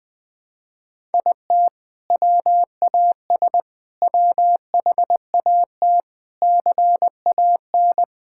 Morse code, heard as a keyed tone, about 21 wpm, 700 Hz. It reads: IT WAS WHAT CAN
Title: Sentences with 6-Word Phrases - Spoken Beginning - 20wpm